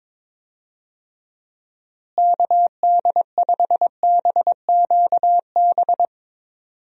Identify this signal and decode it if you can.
Morse code, heard as a keyed tone, about 21 wpm, 700 Hz. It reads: KD5BQB
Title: USA Call Signs - 6 Characters - Repeated 2x - 22wpm